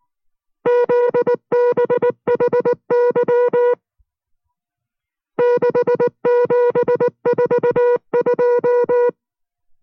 Morse code, heard as a keyed tone, about 19 wpm, 490 Hz. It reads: ZBHY 6742